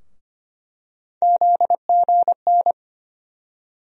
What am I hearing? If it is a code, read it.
Morse code, heard as a keyed tone, about 25 wpm, 700 Hz. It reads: ZGN
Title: USA Call Sign Postfixes - 3 Letters 25wpm